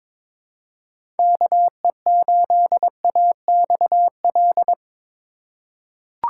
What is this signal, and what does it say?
Morse code, heard as a keyed tone, about 22 wpm, 700 Hz. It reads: KE8AXL
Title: USA Call Signs - 6 Characters - 3x Character Spacing - 22wpm